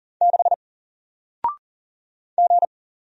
Morse code, heard as a keyed tone, about 40 wpm, 700 Hz. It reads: 6 G
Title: Single Letter-Number 40wpm